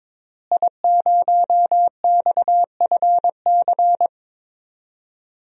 Morse code, heard as a keyed tone, about 22 wpm, 700 Hz. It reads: I0XFC
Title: DXCC-100 Call Signs - 5 Characters - Repeated 3x - 22wpm